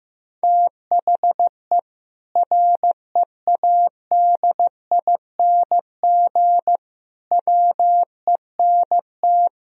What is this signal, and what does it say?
Morse code, heard as a keyed tone, about 15 wpm, 700 Hz. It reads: THE READING WENT
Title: Sentences from Top 1000 Words - Review Part 3 - 15wpm